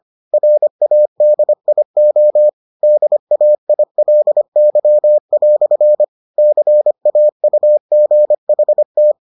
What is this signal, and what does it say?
Morse code, heard as a keyed tone, about 25 wpm, 600 Hz. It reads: RADIO DAILY" CAUGHT